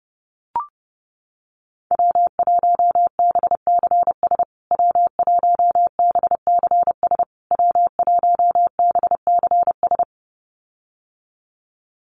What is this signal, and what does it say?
Morse code, heard as a keyed tone, about 30 wpm, 700 Hz. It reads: W1BCS W1BCS W1BCS
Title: USA Call Signs Repeated 3x 30wpm